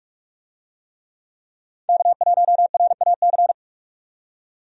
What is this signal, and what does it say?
Morse code, heard as a keyed tone, about 46 wpm, 700 Hz. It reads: K1RAC
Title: USA Call Signs - 5 Characters - 45wpm